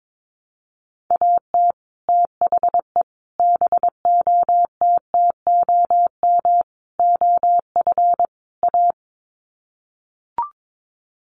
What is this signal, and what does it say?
Morse code, heard as a keyed tone, about 22 wpm, 700 Hz. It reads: AT THE BOTTOM OF A
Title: Sets of 5 Words Encore2 22wpm